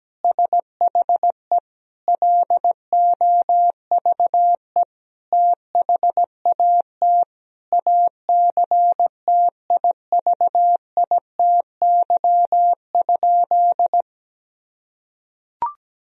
Morse code, heard as a keyed tone, about 17 wpm, 700 Hz. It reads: SHE LOVE THAT ACTIVITY?
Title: Sentences from Top 1000 Words - Review Part 4 - 17wpm